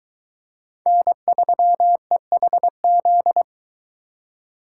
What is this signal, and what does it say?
Morse code, heard as a keyed tone, about 22 wpm, 700 Hz. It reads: N3EHZ